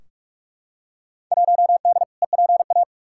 Morse code, heard as a keyed tone, about 46 wpm, 700 Hz. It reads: 1D EPA
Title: Field Day Exchanges - 45wpm